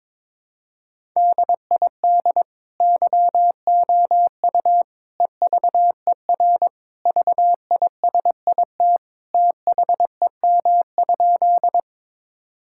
Morse code, heard as a keyed tone, about 22 wpm, 700 Hz. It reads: DID YOU EVER VISIT THEM?
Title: Sentences from Top 1000 Words - Review Part 6 - 22wpm